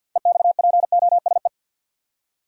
Morse code, heard as a keyed tone, about 52 wpm, 700 Hz. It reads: EXPOSE
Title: Six Letter Words - 50wpm